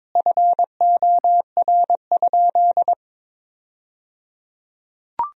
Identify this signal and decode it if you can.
Morse code, heard as a keyed tone, about 22 wpm, 700 Hz. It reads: FOR?
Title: 3x Word-Spacing - Sentences from Top 100 Words - 22wpm